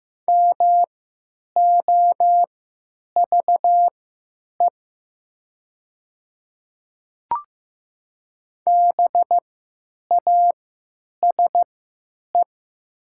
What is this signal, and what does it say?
Morse code, heard as a keyed tone, about 15 wpm, spaced as for 9 wpm, 700 Hz. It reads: MOVE BASE
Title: 3x Character Spacing - Top 300 Words - 15wpm